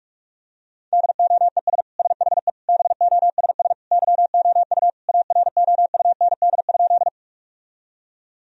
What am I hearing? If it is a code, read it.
Morse code, heard as a keyed tone, about 41 wpm, 700 Hz. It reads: DOES SHE BOSS YOU AROUND?
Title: Sentences from Top 1000 Words - Review Part 4 - 45wpm